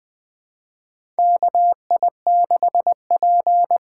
Morse code, heard as a keyed tone, about 20 wpm, 700 Hz. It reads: KI6P
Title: USA Call Signs - 6 Characters - 6x Character Spacing - 20wpm